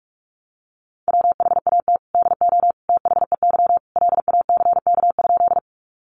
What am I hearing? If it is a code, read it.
Morse code, heard as a keyed tone, about 44 wpm, 700 Hz. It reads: WHAT DO THEY LACK?
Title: Sentences from Top 1100 Words 45wpm